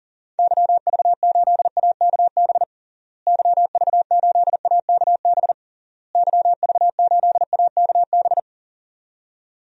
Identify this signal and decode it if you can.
Morse code, heard as a keyed tone, about 40 wpm, 700 Hz. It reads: YV8AKB YV8AKB YV8AKB
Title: DXCC-100 Call Signs - 6 Characters - Repeated 3x - 40wpm